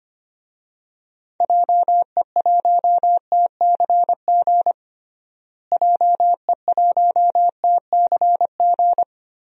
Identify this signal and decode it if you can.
Morse code, heard as a keyed tone, about 25 wpm, 700 Hz. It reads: JE1TCG JE1TCG
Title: DXCC-100 Call Signs - 6 Characters - Repeated 3x - 25wpm